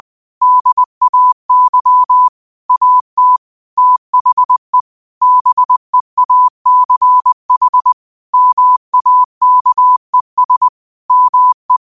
Morse code, heard as a keyed tone, about 20 wpm, 1000 Hz. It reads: DAY AT THE BEACH MAKES ME
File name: qotdfiles / 20wpm